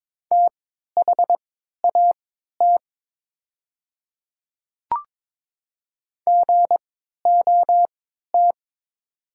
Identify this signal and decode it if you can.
Morse code, heard as a keyed tone, about 22 wpm, spaced as for 12 wpm, 690 Hz. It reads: THAT GOT
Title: 3x Character Spacing - Top 300 Words - 22wpm